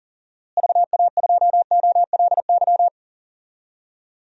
Morse code, heard as a keyed tone, about 40 wpm, 700 Hz. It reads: VA2OLY
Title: DXCC-100 Call Signs - 6 Characters - Repeated 2x - 40wpm